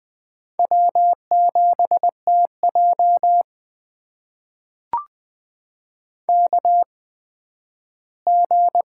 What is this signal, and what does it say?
Morse code, heard as a keyed tone, about 20 wpm, spaced as for 12 wpm, 700 Hz. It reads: W7TJ KG
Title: USA Call Signs - 4 Characters - 8x Character Spacing - 20wpm